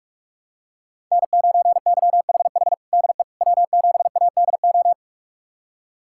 Morse code, heard as a keyed tone, about 42 wpm, 700 Hz. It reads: N9YHH DE W7ADO
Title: USA Callsign DE Callsign - 5 Characters - 45wpm